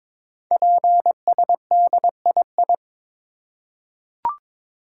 Morse code, heard as a keyed tone, about 22 wpm, 700 Hz. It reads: PSDII
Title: Code Groups - 5 Letter:Numbers - 22wpm